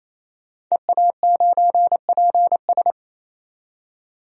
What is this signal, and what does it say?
Morse code, heard as a keyed tone, about 28 wpm, 700 Hz. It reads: EA9PS
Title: DXCC-100 Call Signs - 5 Characters - 28wpm